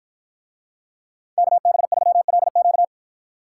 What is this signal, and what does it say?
Morse code, heard as a keyed tone, about 53 wpm, 700 Hz. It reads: KB3LQ